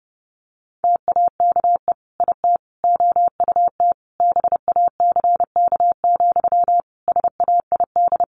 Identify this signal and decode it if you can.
Morse code, heard as a keyed tone, about 30 wpm, 700 Hz. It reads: TAKE IT OUT BACK, SAID
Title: Sentences from Top 1500 Words 30wpm